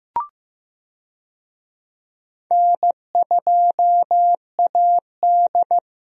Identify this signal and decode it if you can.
Morse code, heard as a keyed tone, about 15 wpm, 700 Hz. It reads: N2AD
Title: USA Call Signs - 4 Characters - Repeated 3x - 15wpm